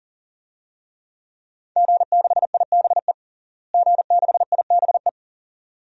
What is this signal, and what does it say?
Morse code, heard as a keyed tone, about 37 wpm, 700 Hz. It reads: G6IBE G6IBE
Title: DXCC-100 Call Signs - 5 Characters - Repeated 2x - 40wpm